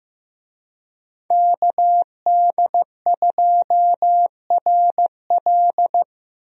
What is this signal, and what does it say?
Morse code, heard as a keyed tone, about 15 wpm, 700 Hz. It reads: KD2RL
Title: USA Call Signs - 5 Characters - Repeated 2x - 15wpm